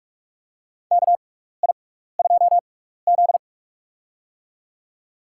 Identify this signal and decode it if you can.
Morse code, heard as a keyed tone, about 44 wpm, spaced as for 15 wpm, 700 Hz. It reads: KI2Z
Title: USA Call Signs - 4 Characters - 6x Character Spacing - 45wpm